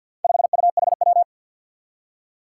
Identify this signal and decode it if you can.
Morse code, heard as a keyed tone, about 53 wpm, 700 Hz. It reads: 5UHW
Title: Four Challenging Letter-Number Combinations 50wpm